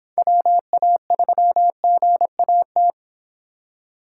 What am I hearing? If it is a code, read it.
Morse code, heard as a keyed tone, about 29 wpm, 700 Hz. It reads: WA3GAT